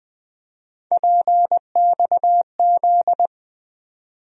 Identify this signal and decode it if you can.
Morse code, heard as a keyed tone, about 20 wpm, 700 Hz. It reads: PXZ
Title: USA Call Sign Postfixes - 3 Letters 20wpm